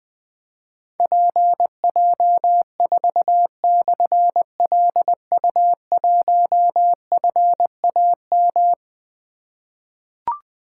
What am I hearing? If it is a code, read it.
Morse code, heard as a keyed tone, about 21 wpm, 700 Hz. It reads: PJ4/LU1FAM
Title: Difficult Call Signs 20wpm